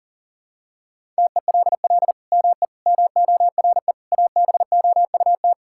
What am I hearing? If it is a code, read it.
Morse code, heard as a keyed tone, about 40 wpm, 700 Hz. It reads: TELL ME MORE ABOUT
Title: Sentences from Top 1000 Words - Review Part 6 - 40wpm